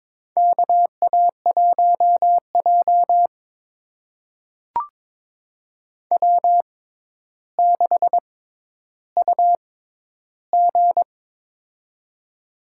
Morse code, heard as a keyed tone, about 22 wpm, spaced as for 11 wpm, 700 Hz. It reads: KA1J W6UG
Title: USA Call Signs - 4 Characters - 6x Character Spacing - 22wpm